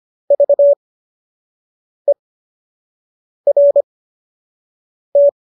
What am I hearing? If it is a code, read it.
Morse code, heard as a keyed tone, about 25 wpm, spaced as for 6 wpm, 570 Hz. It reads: VERT
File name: qso003 06